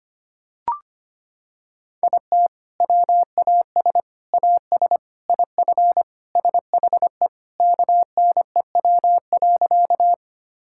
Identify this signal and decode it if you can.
Morse code, heard as a keyed tone, about 25 wpm, 700 Hz. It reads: IT WAS AS IF SHE KNEW.
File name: practice / 25wpm / Sentences with 4-Word Phrases - 25wpm